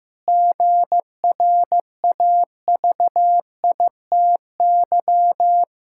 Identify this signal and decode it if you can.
Morse code, heard as a keyed tone, about 15 wpm, 700 Hz. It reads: GRAVITY
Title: Pangram Challenge v2 - 15wpm